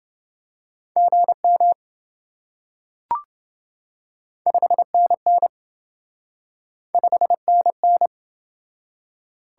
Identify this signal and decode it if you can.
Morse code, heard as a keyed tone, about 27 wpm, 700 Hz. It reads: GM 5NN 5NN